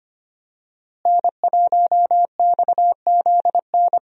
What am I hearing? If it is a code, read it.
Morse code, heard as a keyed tone, about 25 wpm, 700 Hz. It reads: N1XZN
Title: USA Call Signs - 5 Characters - Repeated 2x - 25wpm